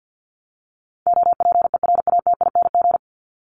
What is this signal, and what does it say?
Morse code, heard as a keyed tone, about 50 wpm, 700 Hz. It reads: OPERATING
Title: ING Suffix - Top 100 Words with Base Words 50wpm